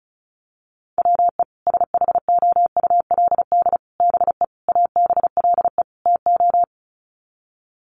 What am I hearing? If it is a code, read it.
Morse code, heard as a keyed tone, about 34 wpm, 700 Hz. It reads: WE SHOULD BE ABLE TO
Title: Sets of 5 Words Encore2 35wpm